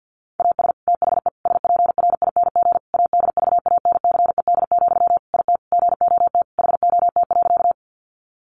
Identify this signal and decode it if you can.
Morse code, heard as a keyed tone, about 50 wpm, 700 Hz. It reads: AS THE SPRING ADVANCED, IT GOT HOT.
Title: Sentences from Top 1200 Words 50wpm